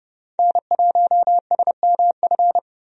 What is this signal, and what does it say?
Morse code, heard as a keyed tone, about 32 wpm, 700 Hz. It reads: N1SMF
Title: USA Call Signs - 5 Characters - 30wpm